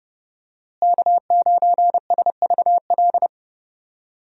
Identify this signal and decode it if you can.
Morse code, heard as a keyed tone, about 32 wpm, 700 Hz. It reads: K9SVL
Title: USA Call Signs - 5 Characters - Repeated 3x - 30wpm